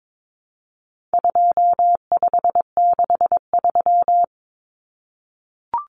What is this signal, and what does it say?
Morse code, heard as a keyed tone, about 22 wpm, 700 Hz. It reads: 2563
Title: Common Cut Numbers - 4-Digit 22wpm